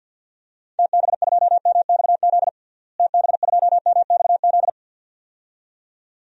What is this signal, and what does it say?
Morse code, heard as a keyed tone, about 48 wpm, 700 Hz. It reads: TB2MXZ TB2MXZ
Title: DXCC-100 Call Signs - 6 Characters - Repeated 2x - 50wpm